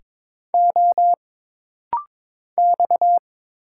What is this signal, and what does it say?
Morse code, heard as a keyed tone, about 22 wpm, 700 Hz. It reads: O X
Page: This rapid-fire timing is geared to push your ICR (Instant Character Recognition) skills.